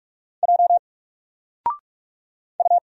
Morse code, heard as a keyed tone, about 46 wpm, 700 Hz. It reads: J U
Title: Single Letter-Number 45wpm